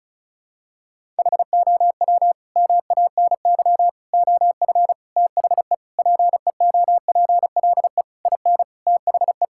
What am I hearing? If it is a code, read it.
Morse code, heard as a keyed tone, about 36 wpm, 700 Hz. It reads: HOW MANY OF THE PEOPLE IN THE
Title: Sentences from Top 1000 Words - Review Part 6 - 35wpm